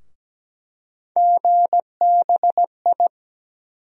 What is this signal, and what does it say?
Morse code, heard as a keyed tone, about 17 wpm, 700 Hz. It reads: GBI
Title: USA Call Sign Postfixes - 3 Letters 17wpm